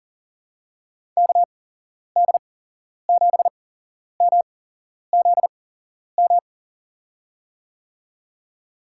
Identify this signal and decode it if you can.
Morse code, heard as a keyed tone, about 38 wpm, spaced as for 11 wpm, 700 Hz. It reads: KD7MZM
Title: USA Call Signs - 8x Character Spacing - 40wpm